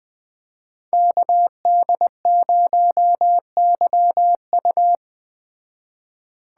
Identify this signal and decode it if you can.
Morse code, heard as a keyed tone, about 20 wpm, 700 Hz. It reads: KD0YU
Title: Speed Racing - USA 2x2 Call Signs 17wpm